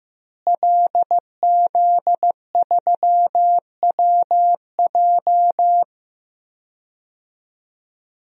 Morse code, heard as a keyed tone, about 15 wpm, 700 Hz. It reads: LZ3WJ
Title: DXCC-100 Call Signs - 5 Characters - Repeated 2x - 15wpm